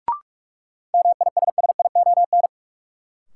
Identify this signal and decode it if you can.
Morse code, heard as a keyed tone, about 43 wpm, 700 Hz. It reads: MISSION
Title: Sentences from Top 1600 Words - New Word Drill Alt - 45wpm